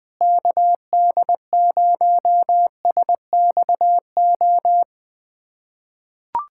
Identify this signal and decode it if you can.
Morse code, heard as a keyed tone, about 20 wpm, 700 Hz. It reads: KD0SXO
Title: USA Callsign DE Callsign 20wpm